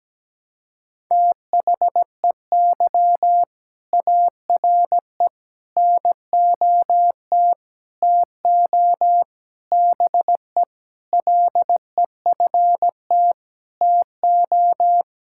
Morse code, heard as a keyed tone, about 17 wpm, 700 Hz. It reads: THEY ARE NOT TO BE LEFT TO
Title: Sentences from Top 1000 Words - Review Part 5 - 17wpm